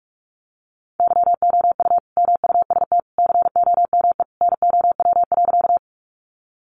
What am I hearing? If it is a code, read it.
Morse code, heard as a keyed tone, about 44 wpm, 700 Hz. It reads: YOU MUST COME NOW.
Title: Sentences from Top 200 Words 45wpm